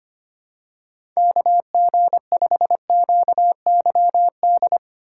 Morse code, heard as a keyed tone, about 25 wpm, 700 Hz. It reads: KG5QYD
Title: USA Call Signs - 6 Characters - 4x Character Spacing - 25wpm